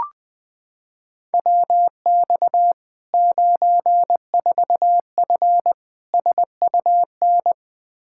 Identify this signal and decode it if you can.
Morse code, heard as a keyed tone, about 21 wpm, 700 Hz. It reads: WX 94F SUN
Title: Practice QSO - Exchange 2 - Info - 20wpm